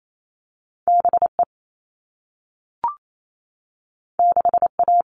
Morse code, heard as a keyed tone, about 28 wpm, 700 Hz. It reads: BE 6A
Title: All Cut Numbers - 2-Digit 28wpm